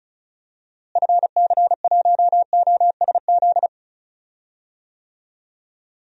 Morse code, heard as a keyed tone, about 34 wpm, 700 Hz. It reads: FC1OSZ
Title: DXCC-100 Call Signs - 6 Characters - Repeated 3x - 35wpm